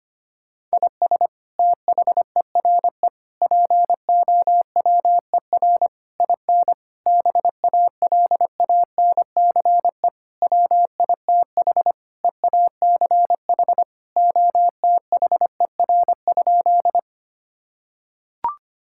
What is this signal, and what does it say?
Morse code, heard as a keyed tone, about 25 wpm, 700 Hz. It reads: IS THERE POWER IN BALANCE WITH EACH OTHER?
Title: Sentences from Top 1000 Words - Review Part 6 - 25wpm